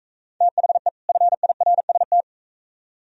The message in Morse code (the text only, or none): THE FIRST